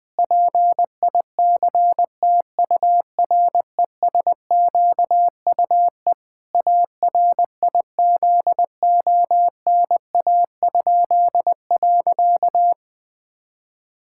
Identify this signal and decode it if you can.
Morse code, heard as a keyed tone, about 20 wpm, 700 Hz. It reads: PICTURESQUE ARIZONA?.
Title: Pangram Challenge v2 - 20wpm